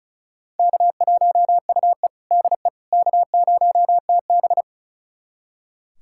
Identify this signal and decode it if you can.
Morse code, heard as a keyed tone, about 36 wpm, 700 Hz. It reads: K1UE DE K0TB
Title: USA Callsign DE Callsign - 4 Characters - 35wpm